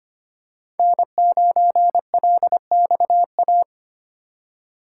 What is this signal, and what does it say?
Morse code, heard as a keyed tone, about 25 wpm, 700 Hz. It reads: N9LXA
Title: USA Call Signs - 5 Characters - 4x Character Spacing - 25wpm